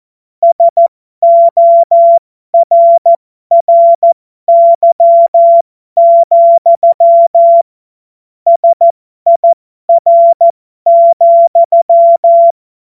Morse code, heard as a keyed tone, about 14 wpm, spaced as for 12 wpm, 680 Hz. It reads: SORRY, SIR,
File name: Morsekurs / Witze